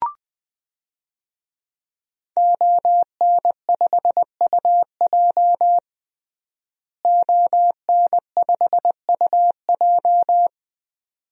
Morse code, heard as a keyed tone, about 21 wpm, 700 Hz. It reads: ON5UJ ON5UJ
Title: DXCC-100 Call Signs - 5 Characters - Repeated 2x - 20wpm